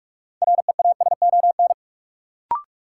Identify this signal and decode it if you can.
Morse code, heard as a keyed tone, about 46 wpm, 700 Hz. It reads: REASON
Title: Sets of 5 Words Encore - New Word Drill - 45wpm